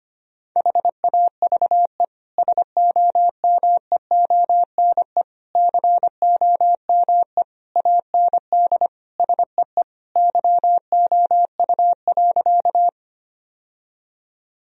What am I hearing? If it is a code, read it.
Morse code, heard as a keyed tone, about 25 wpm, 700 Hz. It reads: HAVE SOMEONE COME AND SEE YOU.
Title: Sentences from Top 1000 Words - Review Part 4 - 25wpm